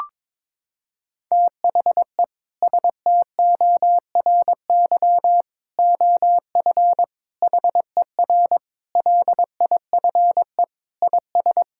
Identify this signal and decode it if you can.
Morse code, heard as a keyed tone, about 22 wpm, 700 Hz. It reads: THE STORY OF HER LIFE IS
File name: practice / 22wpm / Sentences with 3-Word Phrases - 22wpm